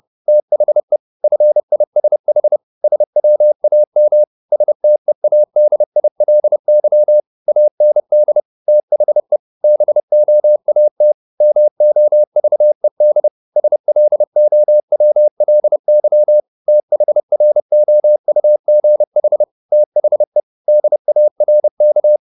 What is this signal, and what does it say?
Morse code, heard as a keyed tone, about 30 wpm, 600 Hz. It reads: THE FISH SWAM STEADILY AND THE BOAT MOVED SLOWLY THROUGH THE DARK
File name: the old man and the sea / the old man and the sea 1 30wpm